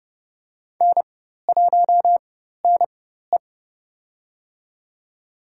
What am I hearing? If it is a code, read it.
Morse code, heard as a keyed tone, about 31 wpm, spaced as for 14 wpm, 700 Hz. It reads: N1NE
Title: USA Call Signs - 4x Character Spacing - 30wpm